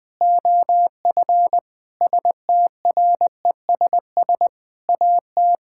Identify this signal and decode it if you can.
Morse code, heard as a keyed tone, about 20 wpm, 700 Hz. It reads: OF STRESS AT
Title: Sentences from Top 1000 Words - Review Part 6 - 20wpm